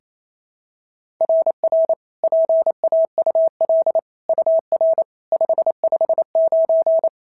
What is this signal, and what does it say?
Morse code, heard as a keyed tone, about 25 wpm, 650 Hz. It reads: RR PAUL UR 559
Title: SOTA QSO Practice 28wpm